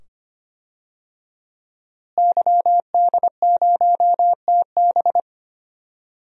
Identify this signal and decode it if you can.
Morse code, heard as a keyed tone, about 25 wpm, 700 Hz. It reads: YD0TB
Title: DXCC-100 Call Signs - 5 Characters - Repeated 2x - 25wpm